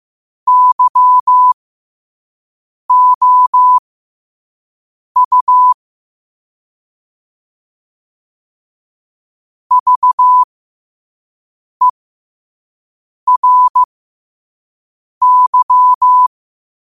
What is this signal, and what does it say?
Morse code, heard as a keyed tone, about 15 wpm, spaced as for 5 wpm, 1000 Hz. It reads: YOU VERY